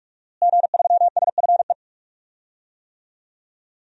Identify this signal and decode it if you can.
Morse code, heard as a keyed tone, about 46 wpm, 700 Hz. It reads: G3SFE
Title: DXCC-100 Call Signs - 5 Characters - Repeated 2x - 45wpm